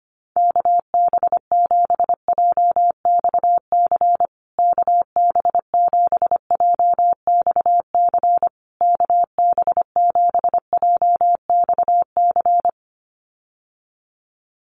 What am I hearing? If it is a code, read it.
Morse code, heard as a keyed tone, about 25 wpm, 700 Hz. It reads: KB7JXC KB7JXC KB7JXC
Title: USA Call Signs Repeated 3x 25wpm